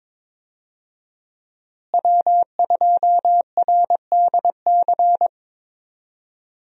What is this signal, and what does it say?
Morse code, heard as a keyed tone, about 22 wpm, 700 Hz. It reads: W2RDC
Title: USA Call Signs - 5 Characters - Repeated 3x - 22wpm